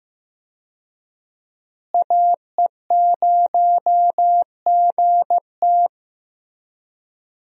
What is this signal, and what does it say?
Morse code, heard as a keyed tone, about 15 wpm, 700 Hz. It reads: AE0GT
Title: USA Call Signs - 5 Characters - Repeated 2x - 15wpm